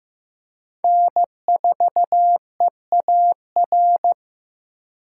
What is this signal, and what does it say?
Morse code, heard as a keyed tone, about 18 wpm, 700 Hz. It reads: N4EAR
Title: Speed Racing - USA 1x3 Call Signs 15wpm